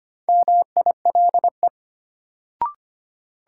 Morse code, heard as a keyed tone, about 25 wpm, 700 Hz. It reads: MILE
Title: Sentences from Top 400 Words - New Word Drill - 25wpm